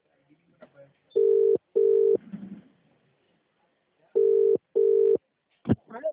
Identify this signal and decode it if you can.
Telephone call progress tone (UK), heard as a telephone network's tone, about 400, 450 Hz, with speech. Ringback tone.